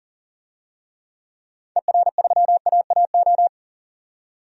Morse code, heard as a keyed tone, about 40 wpm, 700 Hz. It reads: ER3AAO
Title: DXCC-100 Call Signs - 6 Characters - 40wpm